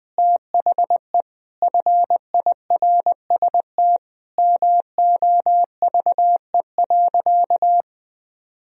Morse code, heard as a keyed tone, about 21 wpm, 700 Hz. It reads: THE FIRST MOVE.
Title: Sentences from Top 1000 Words - Review Part 1 - 20wpm